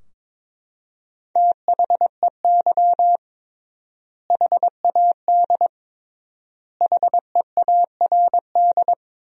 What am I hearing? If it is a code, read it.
Morse code, heard as a keyed tone, about 22 wpm, 700 Hz. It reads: THEY HAD HEARD
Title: 3x Word-Spacing - Sentences from Top 400 Words - 22wpm